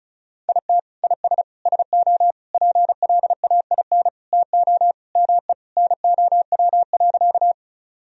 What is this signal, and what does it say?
Morse code, heard as a keyed tone, about 36 wpm, 700 Hz. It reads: IT IS SO PLAIN TO ME NOW.
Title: Sentences from Top 1000 Words - Review Part 2 - 35wpm